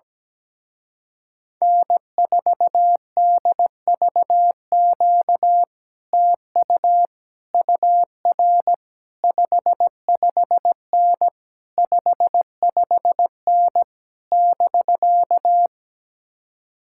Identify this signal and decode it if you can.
Morse code, heard as a keyed tone, about 17 wpm, 700 Hz. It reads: N4DVQ TU UR 55N 55N <BK>
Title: Speed-Racing Common QSO Phrases - 17wpm